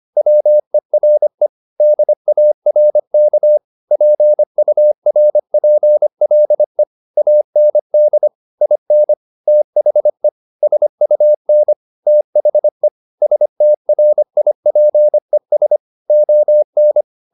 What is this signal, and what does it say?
Morse code, heard as a keyed tone, about 25 wpm, 600 Hz. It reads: WERE DARK PURPLE AND IN THE SUN THE STRIPES ON
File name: the old man and the sea part 1 25wpm